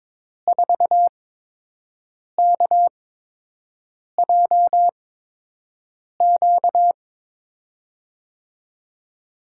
Morse code, heard as a keyed tone, about 22 wpm, spaced as for 6 wpm, 700 Hz. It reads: 4KJQ
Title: USA Call Signs - 8x Character Spacing - 22wpm